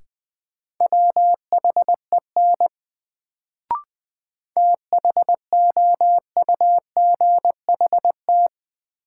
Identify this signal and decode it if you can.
Morse code, heard as a keyed tone, about 20 wpm, 700 Hz. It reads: WHEN THOUGHT
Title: Sets of 2 Words - New Word Drill - 20wpm